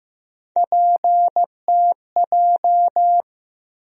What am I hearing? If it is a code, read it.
Morse code, heard as a keyed tone, about 15 wpm, 700 Hz. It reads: PTJ
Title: USA Call Sign Postfixes - 3 Letters 15wpm